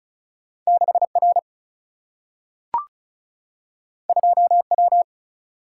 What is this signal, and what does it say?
Morse code, heard as a keyed tone, about 36 wpm, 700 Hz. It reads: 6 R 2 W
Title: Single-Letter Single-Number 35wpm